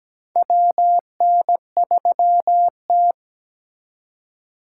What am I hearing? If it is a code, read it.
Morse code, heard as a keyed tone, about 17 wpm, 700 Hz. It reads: WN3T
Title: USA Call Signs - 4 Characters - 4x Character Spacing - 17wpm